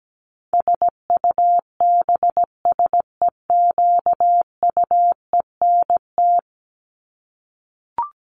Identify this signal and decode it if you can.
Morse code, heard as a keyed tone, about 17 wpm, 700 Hz. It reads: SUBSEQUENT
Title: LY Suffix - Top 100 Words with Base Words 17wpm